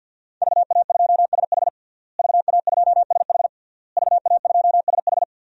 Morse code, heard as a keyed tone, about 53 wpm, 700 Hz. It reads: VA2SH VA2SH VA2SH
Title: DXCC-100 Call Signs - 5 Characters - Repeated 3x - 50wpm